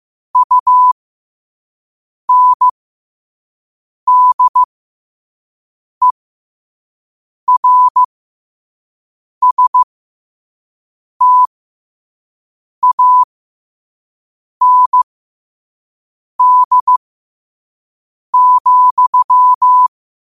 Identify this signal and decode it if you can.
Morse code, heard as a keyed tone, about 15 wpm, spaced as for 5 wpm, 1000 Hz. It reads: UNDERSTAND,
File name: qotdfiles / 5wpm